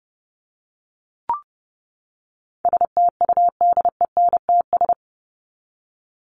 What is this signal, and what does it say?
Morse code, heard as a keyed tone, about 30 wpm, 700 Hz. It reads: STUDENTS
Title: ES:S Suffix - Top 100 Words with Base Words 30wpm